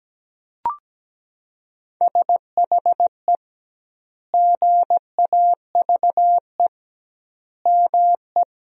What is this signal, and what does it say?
Morse code, heard as a keyed tone, about 17 wpm, 700 Hz. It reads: SHE GAVE ME
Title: 2x Word-Spacing - Sentences from Top 500 Words - 17wpm